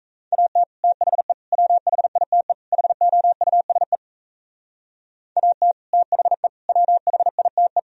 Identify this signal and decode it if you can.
Morse code, heard as a keyed tone, about 33 wpm, 700 Hz. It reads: AT THE WHITE HOUSE AT THE WHITE
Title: Speed-Racing - Sets of 4 Words 28wpm